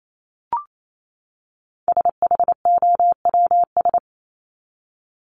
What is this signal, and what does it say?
Morse code, heard as a keyed tone, about 28 wpm, 700 Hz. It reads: SHOWS
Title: ES:S Suffix - Top 100 Words with Base Words 28wpm